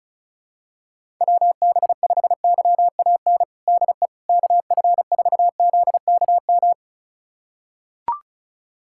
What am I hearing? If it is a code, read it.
Morse code, heard as a keyed tone, about 36 wpm, 700 Hz. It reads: WB5YAN DE KF4ZKM
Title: USA Callsign DE Callsign 35wpm